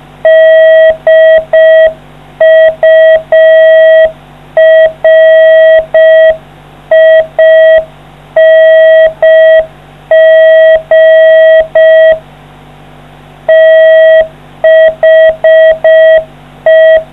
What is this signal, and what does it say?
Morse code, heard as a keyed tone, about 4 wpm, 640 Hz. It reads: DURING THE